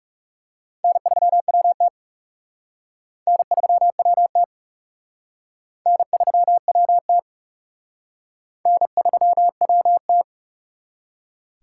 Morse code, heard as a keyed tone, about 35 wpm, 700 Hz. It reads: N3WT N3WT N3WT N3WT
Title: Speed Racing - USA 1x2 Call Signs 30wpm